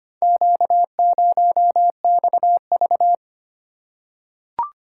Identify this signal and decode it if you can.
Morse code, heard as a keyed tone, about 25 wpm, 700 Hz. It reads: Q0XV
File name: practice / 25wpm / Four Letter-Number Combinations - 25wpm